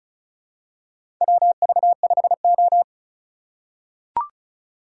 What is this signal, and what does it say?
Morse code, heard as a keyed tone, about 36 wpm, 700 Hz. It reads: WV5O
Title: USA Call Signs - 4 Characters - 2x Character Spacing - 35wpm